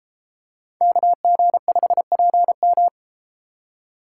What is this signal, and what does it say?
Morse code, heard as a keyed tone, about 32 wpm, 700 Hz. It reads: KG5PM